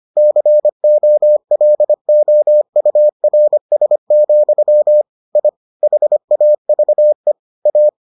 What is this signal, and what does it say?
Morse code, heard as a keyed tone, about 25 wpm, 600 Hz. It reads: COLOURS, I HAVE A